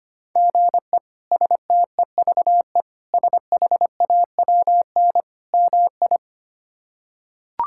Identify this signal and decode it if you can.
Morse code, heard as a keyed tone, about 25 wpm, 700 Hz. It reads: GE STEVE SHAWN MI
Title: K1USN SST QSO Practice - 25wpm